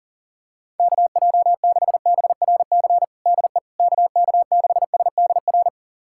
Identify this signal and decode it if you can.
Morse code, heard as a keyed tone, about 40 wpm, 700 Hz. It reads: KJ6BRC DE KK6SDR
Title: USA Callsign DE Callsign - 6 Characters - 40wpm